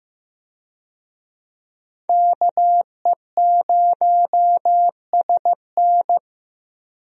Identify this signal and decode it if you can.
Morse code, heard as a keyed tone, about 15 wpm, 700 Hz. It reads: KE0SN
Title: USA Call Signs - 5 Characters - Repeated 3x - 15wpm